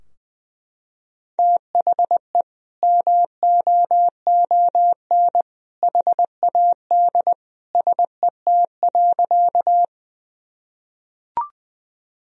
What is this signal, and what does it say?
Morse code, heard as a keyed tone, about 20 wpm, 700 Hz. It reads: THE MOON HAD SET.
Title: Sentences from Top 1000 Words - Review Part 3 - 20wpm